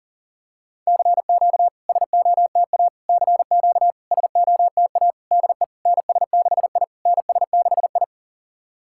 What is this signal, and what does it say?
Morse code, heard as a keyed tone, about 40 wpm, 690 Hz. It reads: CQ SOTA CQ SOTA DE NS6I NS6I
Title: SOTA QSO Practice 40wpm